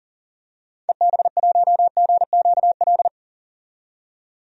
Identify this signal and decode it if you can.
Morse code, heard as a keyed tone, about 38 wpm, 700 Hz. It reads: EB1GQL
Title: DXCC-100 Call Signs - 6 Characters - 40wpm